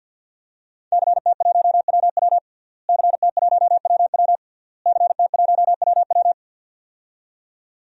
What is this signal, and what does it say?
Morse code, heard as a keyed tone, about 46 wpm, 700 Hz. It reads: CT1WW CT1WW CT1WW
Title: DXCC-100 Call Signs - 5 Characters - Repeated 3x - 50wpm